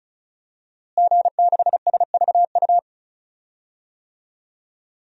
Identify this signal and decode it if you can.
Morse code, heard as a keyed tone, about 34 wpm, 700 Hz. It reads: G6SVU
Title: DXCC-100 Call Signs - 5 Characters - Repeated 3x - 35wpm